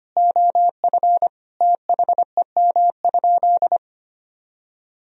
Morse code, heard as a keyed tone, about 25 wpm, 700 Hz. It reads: OF THEM?
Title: Sentences from Top 1000 Words - Review Part 4 - 25wpm